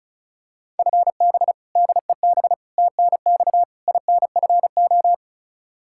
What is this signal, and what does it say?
Morse code, heard as a keyed tone, about 36 wpm, 700 Hz. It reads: FB DEB TNX INFO
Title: Practice QSO - Exchange 3 - More Info - 35wpm